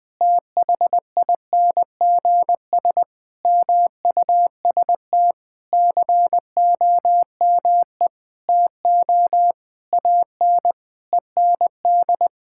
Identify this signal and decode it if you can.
Morse code, heard as a keyed tone, about 20 wpm, 700 Hz. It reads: THINGS MUST COME TO AN END
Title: Sentences with 8-Word Phrases - 20wpm